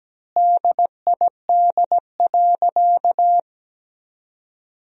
Morse code, heard as a keyed tone, about 17 wpm, 700 Hz. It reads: DID.
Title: 4x Word-Spacing - Sentences from Top 100 Words - 17wpm